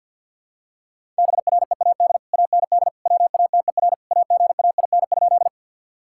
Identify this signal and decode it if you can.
Morse code, heard as a keyed tone, about 50 wpm, 700 Hz. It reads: BREAD AND WATER AGAIN?
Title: Sentences from Top 1600 Words 50wpm